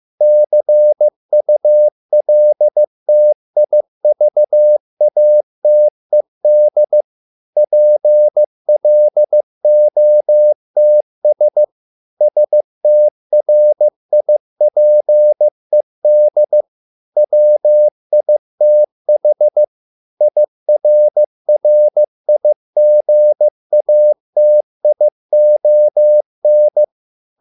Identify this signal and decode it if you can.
Morse code, heard as a keyed tone, about 15 wpm, 600 Hz. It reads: CULTIVATED PLOTS STRIPED WITH IRRIGATION